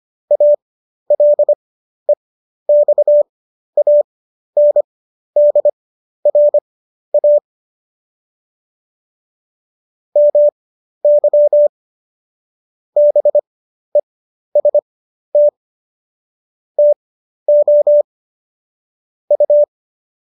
Morse code, heard as a keyed tone, about 25 wpm, spaced as for 12 wpm, 590 Hz. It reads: ALEXANDRA MY BEST TO U
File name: qso012 12